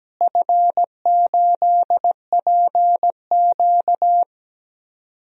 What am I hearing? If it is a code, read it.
Morse code, heard as a keyed tone, about 17 wpm, 700 Hz. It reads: F8PQ
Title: DXCC-100 Call Signs - 4 Characters - Repeated 3x - 17wpm